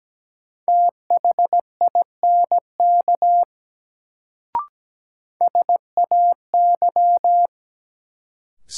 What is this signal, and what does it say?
Morse code, heard as a keyed tone, about 17 wpm, 700 Hz. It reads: THINK SAY
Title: Sets of 2 Words - New Word Drill - 17wpm